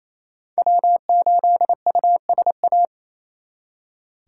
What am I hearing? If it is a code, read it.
Morse code, heard as a keyed tone, about 35 wpm, 700 Hz. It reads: W8USA